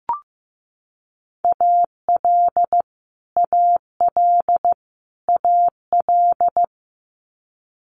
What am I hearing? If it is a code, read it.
Morse code, heard as a keyed tone, about 15 wpm, 700 Hz. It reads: AL AL AL
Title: Multi-letter Phonemes Repeated 3x 15wpm